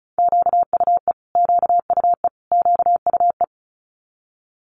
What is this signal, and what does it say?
Morse code, heard as a keyed tone, about 36 wpm, 700 Hz. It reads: QUE QUE QUE
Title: Multi-letter Phonemes Repeated 3x 35wpm